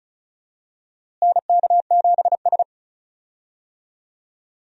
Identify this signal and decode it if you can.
Morse code, heard as a keyed tone, about 36 wpm, 700 Hz. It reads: NK7S
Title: USA Call Signs - 4 Characters - Repeated 3x - 35wpm